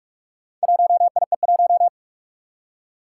Morse code, heard as a keyed tone, about 46 wpm, 700 Hz. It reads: 1IE1
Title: Four Challenging Letter-Number Combinations 45wpm